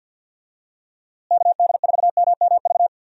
Morse code, heard as a keyed tone, about 49 wpm, 700 Hz. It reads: KD4MMV
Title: USA Call Signs - 6 Characters - Repeated 3x - 50wpm